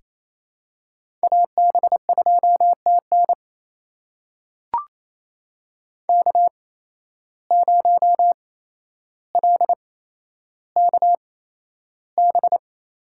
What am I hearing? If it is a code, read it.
Morse code, heard as a keyed tone, about 27 wpm, spaced as for 12 wpm, 700 Hz. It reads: AB2TN K0LKB
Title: USA Call Signs - 8x Character Spacing - 28wpm